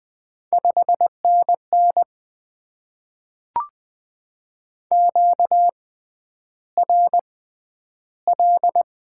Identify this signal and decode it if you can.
Morse code, heard as a keyed tone, about 20 wpm, spaced as for 10 wpm, 690 Hz. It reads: 5NN QRL
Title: Top 20 QSO Elements - 6x Character Spacing - 20wpm